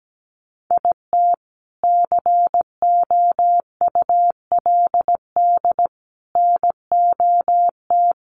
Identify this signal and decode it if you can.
Morse code, heard as a keyed tone, about 17 wpm, 700 Hz. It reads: IT COULD NOT
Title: Sentences from Top 1200 Words 17wpm